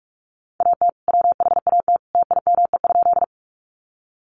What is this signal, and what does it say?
Morse code, heard as a keyed tone, about 44 wpm, 700 Hz. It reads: AT WHAT TIME?
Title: Sentences from Top 100 Words 45wpm